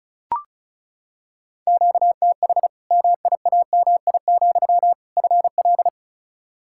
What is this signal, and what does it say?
Morse code, heard as a keyed tone, about 36 wpm, 700 Hz. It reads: QTH MIAMI, FL
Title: Practice QSO - Exchange 1 - Beginning - 35wpm